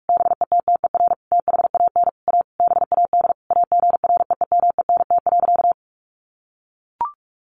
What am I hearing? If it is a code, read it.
Morse code, heard as a keyed tone, about 44 wpm, 700 Hz. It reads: BETTER THAN A BAD AGREEMENT.
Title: Sentences from Top 1500 Words 45wpm